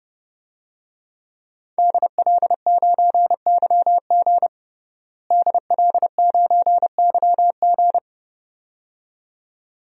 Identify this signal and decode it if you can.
Morse code, heard as a keyed tone, about 30 wpm, 700 Hz. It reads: DL9YG DL9YG
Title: DXCC-100 Call Signs - 5 Characters - Repeated 2x - 30wpm